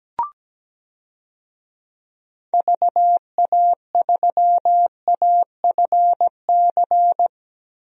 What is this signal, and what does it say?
Morse code, heard as a keyed tone, about 17 wpm, 700 Hz. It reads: VA3AFC
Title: DXCC-100 Call Signs - 6 Characters - 17wpm